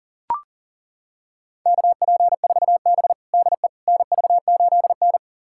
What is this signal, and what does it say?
Morse code, heard as a keyed tone, about 40 wpm, 700 Hz. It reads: KP4B DE NV8N
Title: USA Callsign DE Callsign - 4 Characters - 40wpm